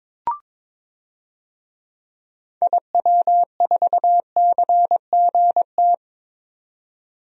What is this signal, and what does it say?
Morse code, heard as a keyed tone, about 22 wpm, 700 Hz. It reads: IW4CGT
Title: DXCC-100 Call Signs - 6 Characters - 22wpm